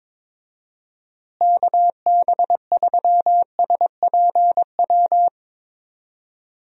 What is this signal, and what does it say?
Morse code, heard as a keyed tone, about 22 wpm, 700 Hz. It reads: KB3SPW
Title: USA Call Signs - 6 Characters - Repeated 3x - 22wpm